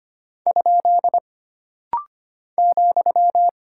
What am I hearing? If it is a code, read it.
Morse code, heard as a keyed tone, about 25 wpm, 700 Hz. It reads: ? ,
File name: Single Punctuation-Prosign - Rapid-Fire - 25wpm